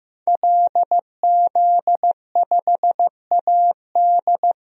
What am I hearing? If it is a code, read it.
Morse code, heard as a keyed tone, about 15 wpm, 700 Hz. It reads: LZ5AD
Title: DXCC-100 Call Signs - 5 Characters - Repeated 2x - 15wpm